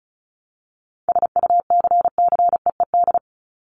Morse code, heard as a keyed tone, about 34 wpm, 700 Hz. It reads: SUCCEED
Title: Top 1501-1750 Words 35wpm